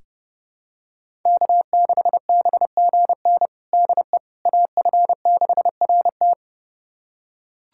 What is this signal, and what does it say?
Morse code, heard as a keyed tone, about 32 wpm, 700 Hz. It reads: K6BGN DE AF6RT
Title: USA Callsign DE Callsign - 5 Characters - 30wpm